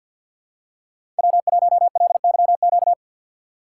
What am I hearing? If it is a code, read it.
Morse code, heard as a keyed tone, about 52 wpm, 700 Hz. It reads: W1LYQ
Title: USA Call Signs - 5 Characters - 8x Character Spacing - 50wpm